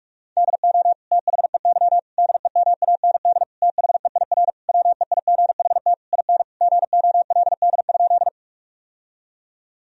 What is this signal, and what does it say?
Morse code, heard as a keyed tone, about 46 wpm, 700 Hz. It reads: DO THEY DEMAND THEIR WEIGHT IN GOLD?
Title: Sentences from Top 1000 Words - Review Part 3 - 45wpm